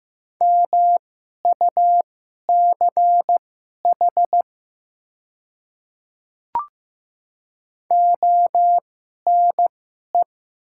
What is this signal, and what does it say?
Morse code, heard as a keyed tone, about 15 wpm, spaced as for 11 wpm, 690 Hz. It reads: MUCH ONE
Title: 2x Character Spacing - Top 200 Words - 15wpm